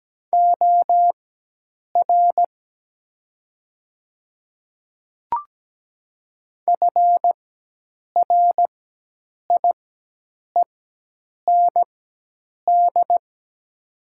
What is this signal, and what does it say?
Morse code, heard as a keyed tone, about 17 wpm, spaced as for 8 wpm, 700 Hz. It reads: OR FRIEND
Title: 4x Character Spacing - Top 300 Words - 17wpm